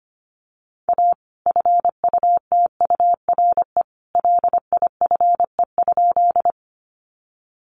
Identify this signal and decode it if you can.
Morse code, heard as a keyed tone, about 25 wpm, 700 Hz. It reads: A FUTURE LIFE?
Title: Sentences from Top 700 Words 25wpm